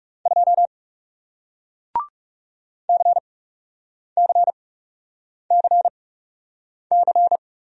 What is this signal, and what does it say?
Morse code, heard as a keyed tone, about 38 wpm, spaced as for 8 wpm, 700 Hz. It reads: 2 CCCC